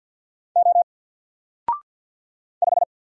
Morse code, heard as a keyed tone, about 48 wpm, 700 Hz. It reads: O 5
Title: Single Letter-Number Alt 50wpm